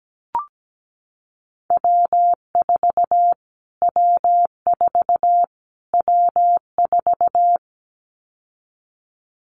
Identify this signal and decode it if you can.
Morse code, heard as a keyed tone, about 17 wpm, 700 Hz. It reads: W4 W4 W4
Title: Callsign Prefixes for Top 25 DXCC - Repeated 3x - 17wpm